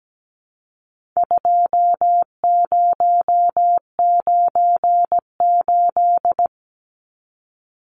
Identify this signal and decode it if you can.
Morse code, heard as a keyed tone, about 17 wpm, 700 Hz. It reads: 2098
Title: Numbers - 4-Digit 17wpm